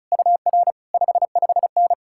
Morse code, heard as a keyed tone, about 48 wpm, spaced as for 42 wpm, 700 Hz. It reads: UR 55N
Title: Speed-Racing Common QSO Phrases Encore - 35wpm